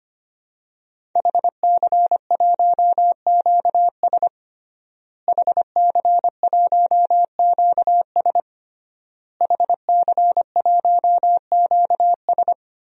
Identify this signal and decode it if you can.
Morse code, heard as a keyed tone, about 25 wpm, 700 Hz. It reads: HC1QS HC1QS HC1QS
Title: DXCC-100 Call Signs - 5 Characters - Repeated 3x - 25wpm